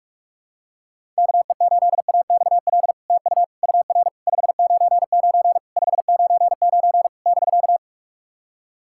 Morse code, heard as a keyed tone, about 46 wpm, 700 Hz. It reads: KE8AXL TU UR 599 599 <BK>
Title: SOTA QSO Practice 45wpm